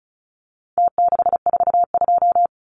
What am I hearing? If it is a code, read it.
Morse code, heard as a keyed tone, about 34 wpm, 700 Hz. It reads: T642
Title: Numbers - Contest Style - 4-Digit 35wpm